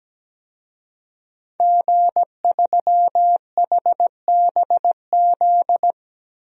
Morse code, heard as a keyed tone, about 17 wpm, 700 Hz. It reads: G3HBZ
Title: DXCC-100 Call Signs - 5 Characters - Repeated 2x - 17wpm